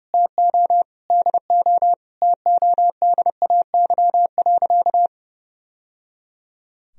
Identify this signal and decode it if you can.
Morse code, heard as a keyed tone, about 30 wpm, 700 Hz. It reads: TO DO TODAY.
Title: Sentences from Top 1000 Words - Review Part 4 - 30wpm